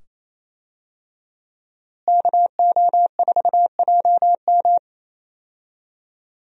Morse code, heard as a keyed tone, about 28 wpm, 700 Hz. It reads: KO4JM
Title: USA Call Signs - 5 Characters - Repeated 2x - 28wpm